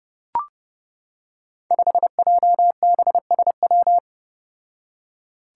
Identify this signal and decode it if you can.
Morse code, heard as a keyed tone, about 30 wpm, 700 Hz. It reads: 5JBSW
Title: Five Challenging Letter-Number Combinations 30wpm